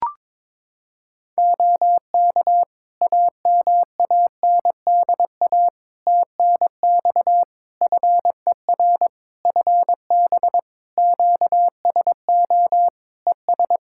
Morse code, heard as a keyed tone, about 22 wpm, 700 Hz. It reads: OK AMANDA TNX FER FB QSO ES
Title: Practice QSO - Exchange 4 - Ending - 22wpm